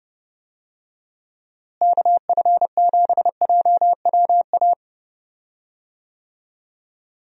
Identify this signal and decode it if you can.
Morse code, heard as a keyed tone, about 30 wpm, 700 Hz. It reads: KF7JWA
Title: USA Call Signs - 6 Characters - Repeated 2x - 30wpm